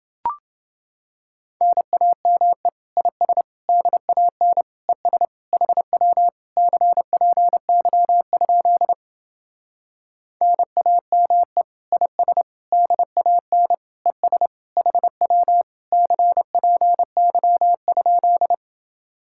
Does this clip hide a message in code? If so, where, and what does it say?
Yes: Morse code from 1.6 to 18.5 s, heard as a keyed tone, about 30 wpm, 700 Hz. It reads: NAME IS DAN ES HW CPY? NAME IS DAN ES HW CPY?